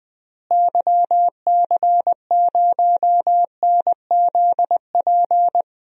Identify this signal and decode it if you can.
Morse code, heard as a keyed tone, about 20 wpm, 700 Hz. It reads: YC0NZP
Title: DXCC-100 Call Signs - 6 Characters - Repeated 2x - 20wpm